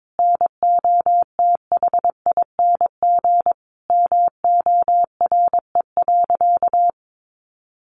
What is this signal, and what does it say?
Morse code, heard as a keyed tone, about 22 wpm, 700 Hz. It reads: NOTHING MORE.
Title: Sentences from Top 800 Words 22wpm